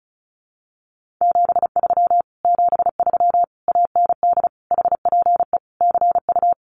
Morse code, heard as a keyed tone, about 34 wpm, 700 Hz. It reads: 73 73 AND HPE CU
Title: Common QSO Phrases Encore 35wpm